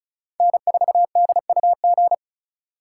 Morse code, heard as a keyed tone, about 36 wpm, 700 Hz. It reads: N4DUG
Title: USA Call Signs - 5 Characters - 4x Character Spacing - 35wpm